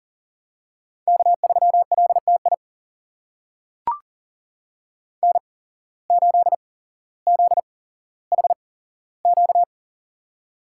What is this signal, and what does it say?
Morse code, heard as a keyed tone, about 40 wpm, spaced as for 17 wpm, 700 Hz. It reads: K3LTI N8ZHQ
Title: USA Call Signs - 5 Characters - 8x Character Spacing - 40wpm